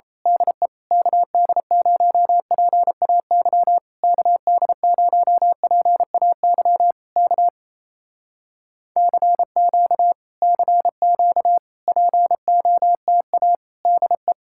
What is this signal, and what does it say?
Morse code, heard as a keyed tone, about 27 wpm, 700 Hz. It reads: DE KD0PAY KD0PAY K CQ CQ POTA DE
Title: Speed-Racing Common QSO Phrases - 28wpm